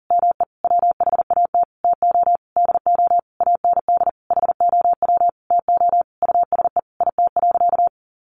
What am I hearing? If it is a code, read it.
Morse code, heard as a keyed tone, about 40 wpm, 700 Hz. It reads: ME WHAT TO DO AND HOW TO USE IT.
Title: Sentences from Top 500 Words 40wpm